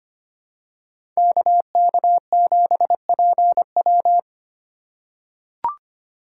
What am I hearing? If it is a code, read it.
Morse code, heard as a keyed tone, about 25 wpm, 700 Hz. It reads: KK7PW
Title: USA Call Signs - 5 Characters - 2x Character Spacing - 25wpm